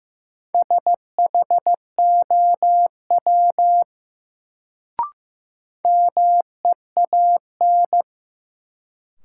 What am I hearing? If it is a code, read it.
Morse code, heard as a keyed tone, about 15 wpm, 700 Hz. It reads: SHOW MEAN
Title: Sentences from Top 200 Words - New Word Drill - 15wpm